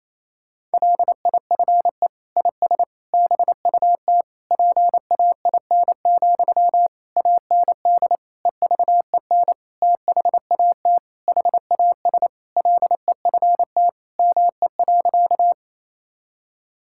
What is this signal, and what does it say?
Morse code, heard as a keyed tone, about 28 wpm, 700 Hz. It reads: LIFE IS BUT PAIN, AND EVEN THAT HAS LEFT ME.
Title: Sentences from Top 1600 Words 28wpm